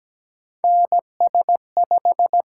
Morse code, heard as a keyed tone, about 17 wpm, 700 Hz. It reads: NS5
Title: USA Call Signs - 4 Characters - 8x Character Spacing - 17wpm